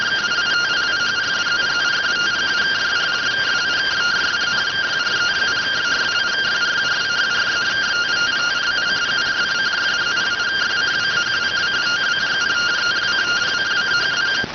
Two-tone teleprinter signal, about 170 Hz shift.